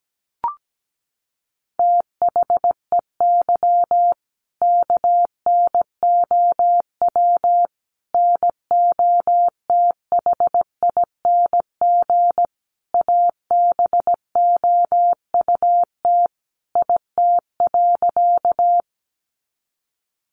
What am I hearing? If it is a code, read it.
Morse code, heard as a keyed tone, about 17 wpm, 700 Hz. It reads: THEY KNOW NOTHING ABOUT IT.
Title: Sentences from Top 500 Words 17wpm